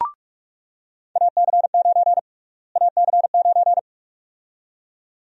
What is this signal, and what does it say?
Morse code, heard as a keyed tone, about 48 wpm, 700 Hz. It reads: AC9 AC9
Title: USA Call Sign Prefixes - Repeated 2x - 45wpm